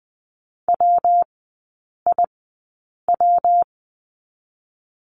Morse code, heard as a keyed tone, about 20 wpm, spaced as for 8 wpm, 700 Hz. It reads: WIW
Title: 114 Lesson 19e - Letter w - Three Characters 20wpm